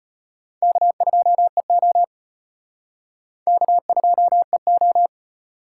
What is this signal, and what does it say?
Morse code, heard as a keyed tone, about 32 wpm, 700 Hz. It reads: K2EO K2EO